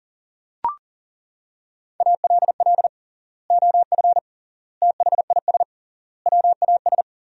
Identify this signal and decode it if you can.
Morse code, heard as a keyed tone, about 40 wpm, 700 Hz. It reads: ALL OF THIS WAS
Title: 3x Word-Spacing - Sentences from Top 500 Words - 40wpm